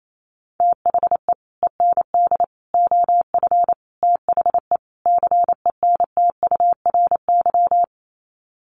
Morse code, heard as a keyed tone, about 28 wpm, 700 Hz. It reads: THE END OF THE CENTURY
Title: Sets of 5 Words Encore2 28wpm